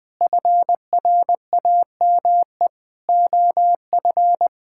Morse code, heard as a keyed tone, about 20 wpm, 700 Hz. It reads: FRAME OF
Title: Sentences from Top 1000 Words - Review Part 6 - 20wpm